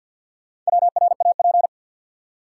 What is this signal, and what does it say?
Morse code, heard as a keyed tone, about 53 wpm, 700 Hz. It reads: WRAP
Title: Four Letter Words - 50wpm